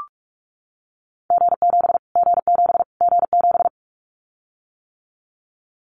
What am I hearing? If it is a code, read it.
Morse code, heard as a keyed tone, about 44 wpm, 700 Hz. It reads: G7 G7 G7
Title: Callsign Prefixes for Top 25 DXCC - Repeated 3x - 45wpm